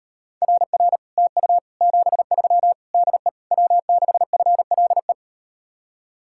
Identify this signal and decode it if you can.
Morse code, heard as a keyed tone, about 38 wpm, 700 Hz. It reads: RR TU 73 DE W6FLE